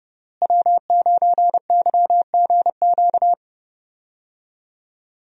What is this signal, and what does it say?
Morse code, heard as a keyed tone, about 29 wpm, 700 Hz. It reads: W9YGQ
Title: USA Call Signs - 5 Characters - Repeated 3x - 30wpm